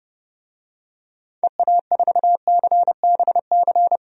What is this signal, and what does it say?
Morse code, heard as a keyed tone, about 32 wpm, 700 Hz. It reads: EA4CBC
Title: DXCC-100 Call Signs - 6 Characters - Repeated 2x - 30wpm